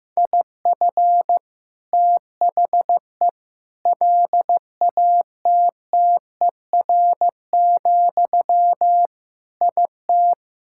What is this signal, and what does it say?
Morse code, heard as a keyed tone, about 15 wpm, 700 Hz. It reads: IF THE LATTER, IT
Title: Sentences from Top 1600 Words 15wpm